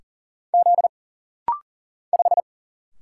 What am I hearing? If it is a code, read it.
Morse code, heard as a keyed tone, about 41 wpm, 700 Hz. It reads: Z 5
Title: Single Letter-Number - Rapid-Fire 40wpm